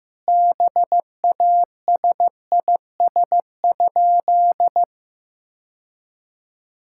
Morse code, heard as a keyed tone, about 15 wpm, 700 Hz. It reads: BASIS?
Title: Sentences from Top 1000 Words - Review Part 5 - 15wpm